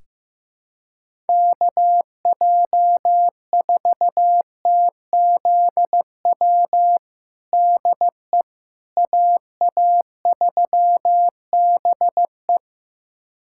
Morse code, heard as a keyed tone, about 16 wpm, 700 Hz. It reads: KJ4TZW DE AA3BE
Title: USA Callsign DE Callsign 15wpm